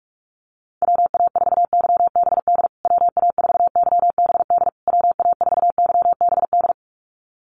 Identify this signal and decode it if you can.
Morse code, heard as a keyed tone, about 44 wpm, 700 Hz. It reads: WA4YBD WA4YBD WA4YBD
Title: USA Call Signs Repeated 3x 45wpm